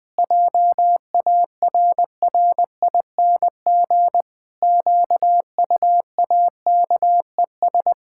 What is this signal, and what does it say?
Morse code, heard as a keyed tone, about 19 wpm, 700 Hz. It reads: JARRING QUAKES
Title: Pangram Challenge v2 - 20wpm